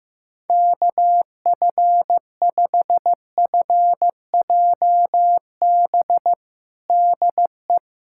Morse code, heard as a keyed tone, about 15 wpm, 700 Hz. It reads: KF5FJB DE
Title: USA Callsign DE Callsign - 6 Characters - 15wpm